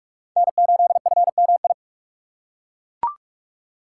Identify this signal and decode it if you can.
Morse code, heard as a keyed tone, about 42 wpm, 700 Hz. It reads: N8FMI
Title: USA Call Signs - 5 Characters - 4x Character Spacing - 45wpm